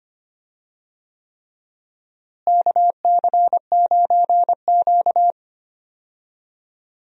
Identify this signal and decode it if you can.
Morse code, heard as a keyed tone, about 25 wpm, 700 Hz. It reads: KC9Q
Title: USA Call Signs - 4 Characters - Repeated 3x - 25wpm